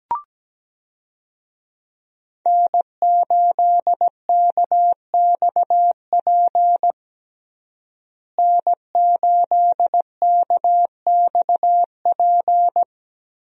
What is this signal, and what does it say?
Morse code, heard as a keyed tone, about 17 wpm, 700 Hz. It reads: N8KXP N8KXP
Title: USA Call Signs - 5 Characters - Repeated 3x - 17wpm